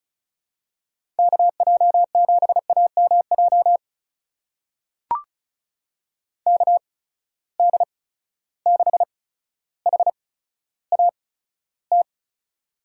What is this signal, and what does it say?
Morse code, heard as a keyed tone, about 34 wpm, spaced as for 15 wpm, 700 Hz. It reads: KJ7AMJ KD6HAT
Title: USA Call Signs - 6 Characters - 8x Character Spacing - 35wpm